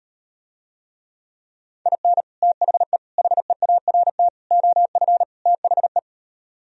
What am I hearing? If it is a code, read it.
Morse code, heard as a keyed tone, about 38 wpm, 700 Hz. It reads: IN THE HEART OF THE